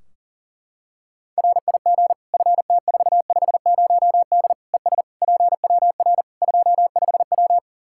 Morse code, heard as a keyed tone, about 40 wpm, 700 Hz. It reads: RIG FT450D ES PWR 25W
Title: Practice QSO - Exchange 2 - Info - 40wpm